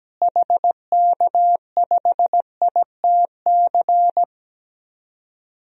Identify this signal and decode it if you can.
Morse code, heard as a keyed tone, about 17 wpm, 700 Hz. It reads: HK5ITC
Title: DXCC-100 Call Signs - 6 Characters - Repeated 2x - 17wpm